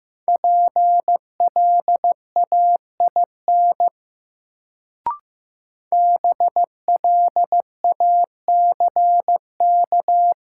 Morse code, heard as a keyed tone, about 15 wpm, 700 Hz. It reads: PLAIN BLACK
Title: Sentences from Top 400 Words - New Word Drill - 15wpm